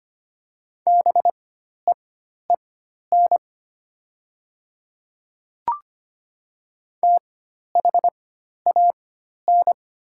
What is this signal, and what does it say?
Morse code, heard as a keyed tone, about 25 wpm, spaced as for 12 wpm, 700 Hz. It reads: BEEN THAN
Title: 4x Character Spacing - Top 300 Words - 25wpm